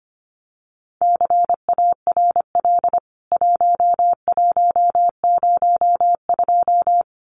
Morse code, heard as a keyed tone, about 25 wpm, 700 Hz. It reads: CARL 1102
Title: CWT Exchange 25wpm